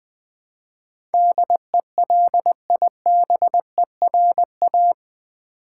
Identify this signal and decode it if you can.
Morse code, heard as a keyed tone, about 20 wpm, 700 Hz. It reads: DELIBERA
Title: Build A Word - 4 Letters At A Time - 12-Letter Words - 20wpm